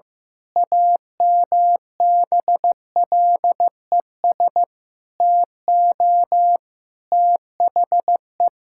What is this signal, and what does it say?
Morse code, heard as a keyed tone, about 16 wpm, 700 Hz. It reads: AMBLES TO THE
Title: Pangram Challenge v2 - 15wpm